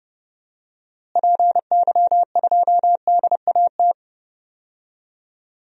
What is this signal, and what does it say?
Morse code, heard as a keyed tone, about 30 wpm, 700 Hz. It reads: PY2DAT
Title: DXCC-100 Call Signs - 6 Characters - Repeated 3x - 30wpm